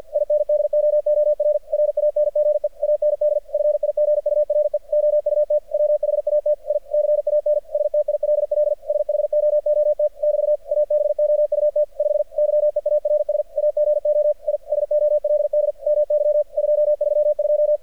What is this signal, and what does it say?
Morse code, heard as a keyed tone, about 50 wpm, 600 Hz, with radio noise. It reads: INDOOR RANGE AND FIGURE OUT WHAT I CAN STILL SHOOT <BT> ABOUT 5 YEARS AGO I SOLD MY 232
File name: archived / CRSnet